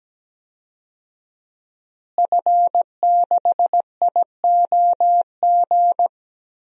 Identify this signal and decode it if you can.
Morse code, heard as a keyed tone, about 17 wpm, 700 Hz. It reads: F6IOG
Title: DXCC-100 Call Signs - 5 Characters - Repeated 2x - 17wpm